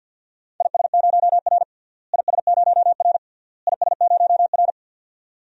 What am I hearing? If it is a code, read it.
Morse code, heard as a keyed tone, about 51 wpm, 700 Hz. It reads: IS0R IS0R IS0R
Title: DXCC-100 Call Signs - 4 Characters - Repeated 3x - 50wpm